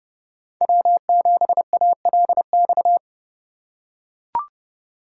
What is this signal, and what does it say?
Morse code, heard as a keyed tone, about 29 wpm, 700 Hz. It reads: W7ALX